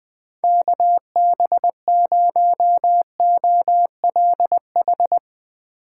Morse code, heard as a keyed tone, about 21 wpm, 700 Hz. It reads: KB0OLH
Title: USA Call Signs - 6 Characters - 8x Character Spacing - 20wpm